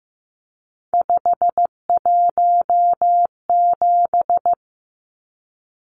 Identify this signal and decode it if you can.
Morse code, heard as a keyed tone, about 15 wpm, 700 Hz. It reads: 517
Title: Numbers - 3-Digit 15wpm